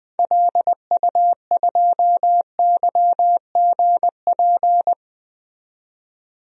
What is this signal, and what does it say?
Morse code, heard as a keyed tone, about 20 wpm, 700 Hz. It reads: LU2YGP
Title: DXCC-100 Call Signs - 6 Characters - 20wpm